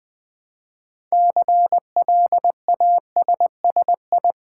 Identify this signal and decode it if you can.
Morse code, heard as a keyed tone, about 20 wpm, 700 Hz. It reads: CLASSI
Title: Build A Word - 2 Letters At A Time - 8-Letter Words - 20wpm